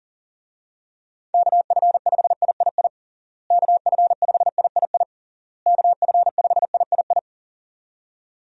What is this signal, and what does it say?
Morse code, heard as a keyed tone, about 37 wpm, 700 Hz. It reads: KF5III KF5III KF5III
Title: USA Call Signs - 6 Characters - Repeated 3x - 40wpm